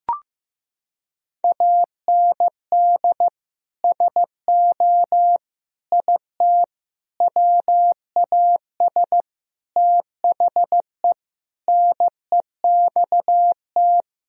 Morse code, heard as a keyed tone, about 15 wpm, 700 Hz. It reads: AND SO IT WAS THE NEXT
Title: Sentences from Top 1000 Words - Review Part 1 - 15wpm